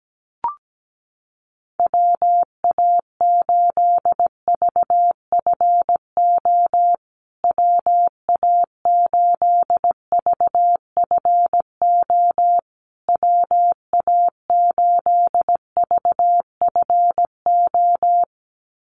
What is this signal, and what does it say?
Morse code, heard as a keyed tone, about 17 wpm, 700 Hz. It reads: WA8VFO WA8VFO WA8VFO
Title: USA Call Signs Repeated 3x 17wpm